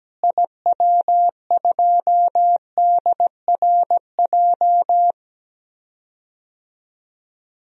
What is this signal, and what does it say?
Morse code, heard as a keyed tone, about 17 wpm, 700 Hz. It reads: IW2DRJ
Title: DXCC-100 Call Signs - 6 Characters - Repeated 3x - 17wpm